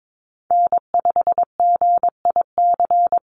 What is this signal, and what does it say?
Morse code, heard as a keyed tone, about 22 wpm, 700 Hz. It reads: N5GIC
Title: USA Call Signs Repeated 2x 22wpm